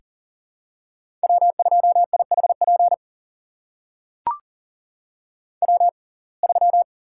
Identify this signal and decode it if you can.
Morse code, heard as a keyed tone, about 40 wpm, spaced as for 29 wpm, 700 Hz. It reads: W2IHP W3
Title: USA Call Signs - 5 Characters - 6x Character Spacing - 40wpm